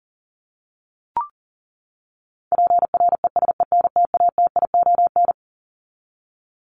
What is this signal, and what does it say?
Morse code, heard as a keyed tone, about 40 wpm, 700 Hz. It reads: PRESENTATION
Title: Top 1501-1750 Words 40wpm